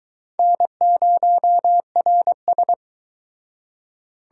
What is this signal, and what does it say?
Morse code, heard as a keyed tone, about 24 wpm, 700 Hz. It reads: N0RS